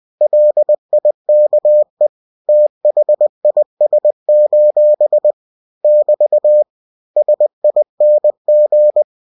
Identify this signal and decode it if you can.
Morse code, heard as a keyed tone, about 20 wpm, 600 Hz. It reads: LIKE THIS: <BT> SING
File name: cw-books / winnie-the-pooh / winnie-20wpm-page-0075